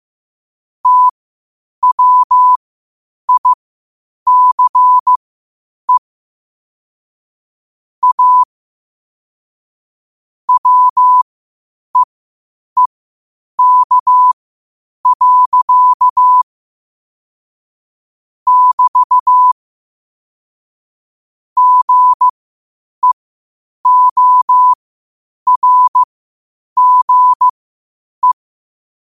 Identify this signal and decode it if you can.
Morse code, heard as a keyed tone, about 15 wpm, spaced as for 8 wpm, 1000 Hz. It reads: TWICE A WEEK. <BT> GEORGE
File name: 8wpm